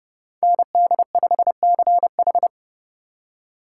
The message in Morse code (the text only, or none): ND5CH